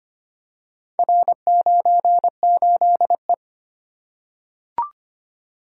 Morse code, heard as a keyed tone, about 25 wpm, 700 Hz. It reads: R98E
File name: practice / 25wpm / Four Letter-Number Combinations - 25wpm